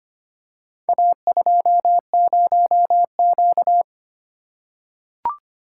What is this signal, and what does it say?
Morse code, heard as a keyed tone, about 25 wpm, 700 Hz. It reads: A20Q
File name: practice / 25wpm / Four Letter-Number Combinations - 25wpm